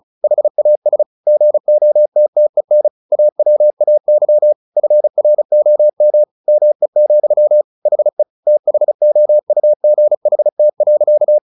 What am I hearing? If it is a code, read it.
Morse code, heard as a keyed tone, about 34 wpm, 600 Hz. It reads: HAS GOTTEN AWAY FROM ME, HE THOUGHT.
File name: the old man and the sea / the old man and the sea 1 35wpm